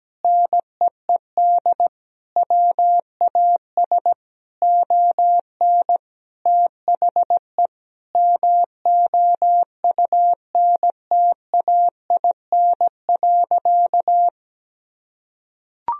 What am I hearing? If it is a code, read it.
Morse code, heard as a keyed tone, about 17 wpm, 700 Hz. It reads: NEED WAS ON THE MOUNTAIN.
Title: Sentences from Top 1000 Words - Review Part 1 - 17wpm